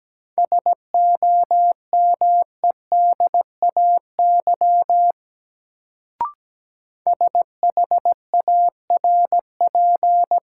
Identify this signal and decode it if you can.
Morse code, heard as a keyed tone, about 17 wpm, 700 Hz. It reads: SOMEDAY SHARP
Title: Sentences from Top 1600 Words - New Word Drill Alt - 17wpm